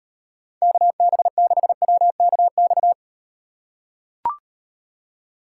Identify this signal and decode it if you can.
Morse code, heard as a keyed tone, about 38 wpm, 700 Hz. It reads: KB6WKX